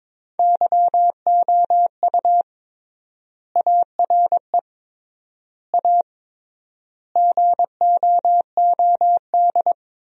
Morse code, heard as a keyed tone, about 21 wpm, 700 Hz. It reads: YOU ARE A GOOD
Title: 3x Word-Spacing - Sentences from Top 200 Words - 22wpm